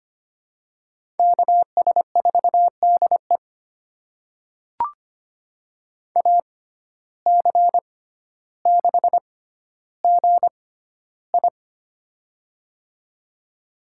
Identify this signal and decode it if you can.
Morse code, heard as a keyed tone, about 25 wpm, spaced as for 13 wpm, 700 Hz. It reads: KS4DE AC6GI
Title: USA Call Signs - 5 Characters - 6x Character Spacing - 25wpm